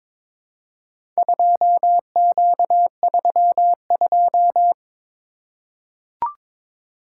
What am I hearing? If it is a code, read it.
Morse code, heard as a keyed tone, about 22 wpm, 700 Hz. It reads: 2Q32
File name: practice / 22wpm / Four Letter-Number Combinations - 22wpm